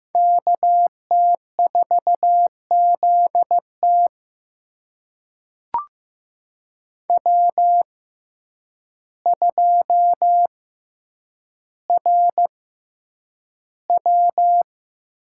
Morse code, heard as a keyed tone, about 15 wpm, spaced as for 8 wpm, 700 Hz. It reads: KT4ZT W2RW
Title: USA Call Signs - 5 Characters - 6x Character Spacing - 15wpm